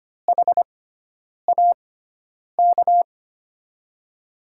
Morse code, heard as a keyed tone, about 25 wpm, spaced as for 9 wpm, 700 Hz. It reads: HAK
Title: USA Call Signs - 6 Characters - 6x Character Spacing - 25wpm